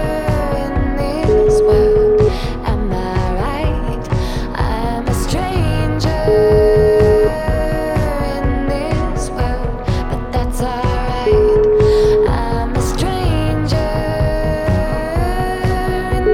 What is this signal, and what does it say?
Ringback tone (Europe, 425 Hz), with noise.